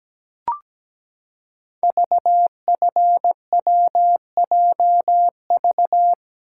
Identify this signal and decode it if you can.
Morse code, heard as a keyed tone, about 17 wpm, 700 Hz. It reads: VFWJV
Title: Code Groups - 5 Letters - 17wpm